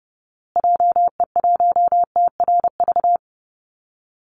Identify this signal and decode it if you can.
Morse code, heard as a keyed tone, about 30 wpm, 700 Hz. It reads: JE1TRV
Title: Call Signs Repeated 2x - 30wpm